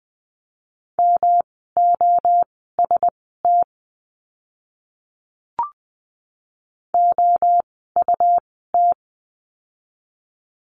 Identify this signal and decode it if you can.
Morse code, heard as a keyed tone, about 20 wpm, spaced as for 14 wpm, 700 Hz. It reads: MOST OUT
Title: 115 Lesson 19f - Letter w - Words 20wpm